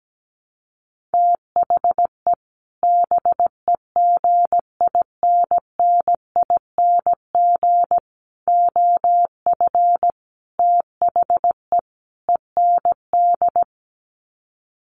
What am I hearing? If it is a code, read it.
Morse code, heard as a keyed tone, about 17 wpm, 700 Hz. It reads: THE BEGINNING OF THE END
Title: Sets of 5 Words Encore2 17wpm